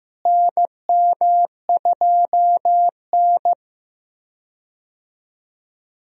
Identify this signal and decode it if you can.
Morse code, heard as a keyed tone, about 15 wpm, 700 Hz. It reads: NM2N
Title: USA Call Signs - 4 Characters - Repeated 3x - 15wpm